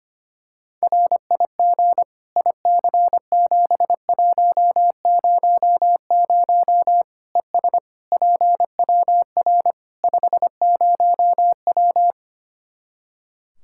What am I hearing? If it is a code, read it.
Morse code, heard as a keyed tone, about 25 wpm, 700 Hz. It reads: RIG IC7100 ES PWR 50W
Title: Practice QSO - Exchange 2 - Info - 25wpm